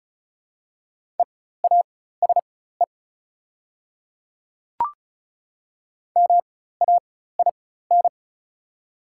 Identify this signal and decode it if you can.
Morse code, heard as a keyed tone, about 36 wpm, spaced as for 16 wpm, 690 Hz. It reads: EASE MAIN
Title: 4x Character Spacing - Top 500 Words - 35wpm